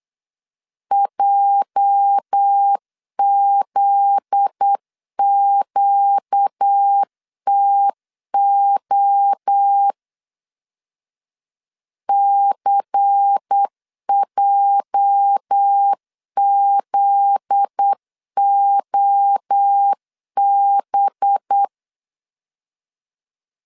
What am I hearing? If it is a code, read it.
Morse code, heard as a keyed tone, about 8 wpm, 800 Hz. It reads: JZQTO CJZOB
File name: LEZIONI / 8 YJ45 / 8 YJ 8wpm 3313